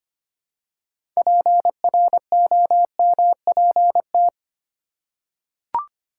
Morse code, heard as a keyed tone, about 25 wpm, 700 Hz. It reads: PROMPT
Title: Six Letter Words - 25wpm